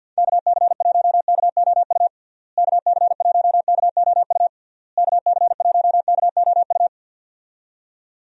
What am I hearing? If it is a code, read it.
Morse code, heard as a keyed tone, about 50 wpm, 700 Hz. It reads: KC1KOU KC1KOU KC1KOU
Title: USA Call Signs - 6 Characters - Repeated 3x - 50wpm